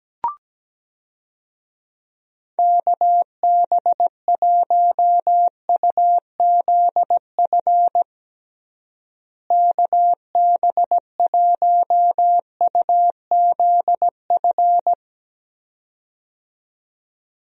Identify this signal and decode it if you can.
Morse code, heard as a keyed tone, about 17 wpm, 700 Hz. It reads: KB1UZF KB1UZF
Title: USA Call Signs - 6 Characters - Repeated 2x - 17wpm